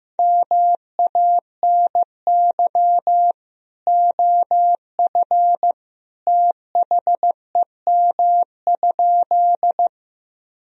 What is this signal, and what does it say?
Morse code, heard as a keyed tone, about 15 wpm, 700 Hz. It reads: MANY OF THEM?
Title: Sentences from Top 1000 Words - Review Part 1 - 15wpm